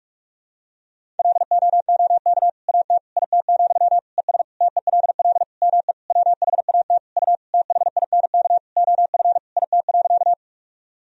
Morse code, heard as a keyed tone, about 41 wpm, 700 Hz. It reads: LOOK AT IT, ES TELL ME WHAT U THINK OF IT.
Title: Sentences from Top 200 Words and Abbreviations 45wpm